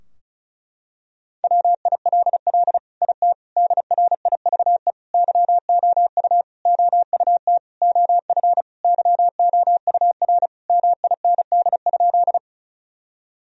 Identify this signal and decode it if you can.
Morse code, heard as a keyed tone, about 36 wpm, 700 Hz. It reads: WILL IT DRIVE YOU OUT OF YOUR MIND?
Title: Sentences from Top 1000 Words - Review Part 2 - 35wpm